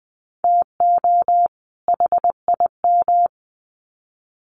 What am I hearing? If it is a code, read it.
Morse code, heard as a keyed tone, about 20 wpm, 700 Hz. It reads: TO HIM
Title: Sets of 2 Words 20wpm